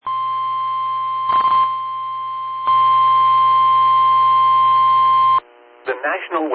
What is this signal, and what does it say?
Emergency Alert System (SAME) tones (the 1050 Hz attention tone), with speech. It is heard from 0.0 to 5.4 s.
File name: eas / KIMB / KIMB